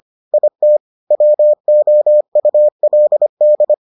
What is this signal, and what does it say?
Morse code, heard as a keyed tone, about 25 wpm, 600 Hz. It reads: IT WOULD